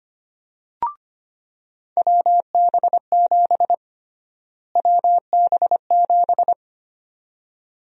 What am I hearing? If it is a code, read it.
Morse code, heard as a keyed tone, about 25 wpm, 700 Hz. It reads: WB7 WB7
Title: USA Call Sign Prefixes - Repeated 2x - 25wpm